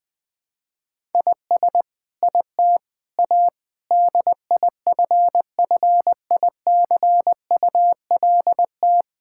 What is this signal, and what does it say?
Morse code, heard as a keyed tone, about 21 wpm, 700 Hz. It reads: IS IT A DIFFICULT
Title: Sentences from Top 1000 Words - Review Part 6 - 20wpm